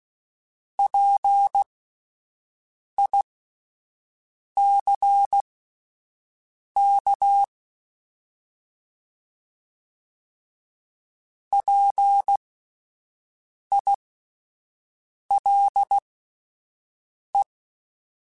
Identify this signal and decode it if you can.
Morse code, heard as a keyed tone, about 16 wpm, spaced as for 5 wpm, 800 Hz. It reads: PICK PILE